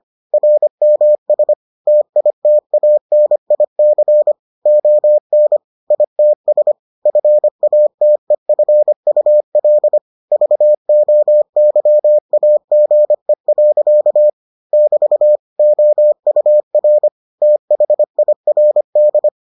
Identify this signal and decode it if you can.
Morse code, heard as a keyed tone, about 25 wpm, 600 Hz. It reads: RMS TITANIC ON ITS FATEFUL VOYAGE. <BT> OUR THIRD